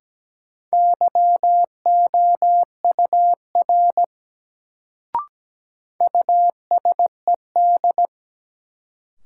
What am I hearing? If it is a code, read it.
Morse code, heard as a keyed tone, about 17 wpm, 700 Hz. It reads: YOUR USED
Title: Sets of 2 Words - New Word Drill - 17wpm